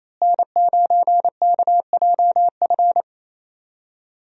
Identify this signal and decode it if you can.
Morse code, heard as a keyed tone, about 28 wpm, 700 Hz. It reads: N9KJF
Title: USA Call Signs - 5 Characters - Repeated 3x - 28wpm